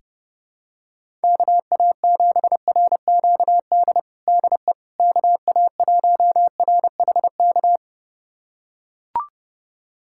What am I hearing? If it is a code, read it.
Morse code, heard as a keyed tone, about 30 wpm, 700 Hz. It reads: KA7RQD DE KA1RHK
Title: USA Callsign DE Callsign - 6 Characters - 30wpm